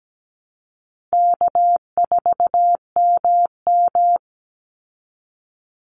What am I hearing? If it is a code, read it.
Morse code, heard as a keyed tone, about 17 wpm, 700 Hz. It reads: K4MM
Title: Easy Call Signs Repeated 3x - 17wpm